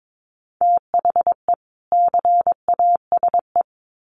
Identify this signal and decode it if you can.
Morse code, heard as a keyed tone, about 22 wpm, 700 Hz. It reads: THE CASE
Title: Sets of 2 Words 22wpm